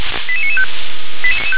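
Five-tone selective call (ZVEI), heard as quick runs of five tones, tones of about 70 ms. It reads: 88995
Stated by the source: ZVEI five-tone selective call received by SDR.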